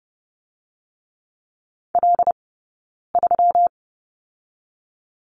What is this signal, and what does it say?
Morse code, heard as a keyed tone, about 32 wpm, 700 Hz. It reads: L 3
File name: Lesson 20d - Number 3 - Two Characters 30wpm